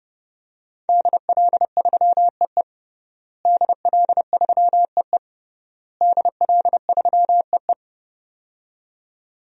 Morse code, heard as a keyed tone, about 30 wpm, 700 Hz. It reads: DL3EE DL3EE DL3EE
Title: DXCC-100 Call Signs - 5 Characters - Repeated 3x - 30wpm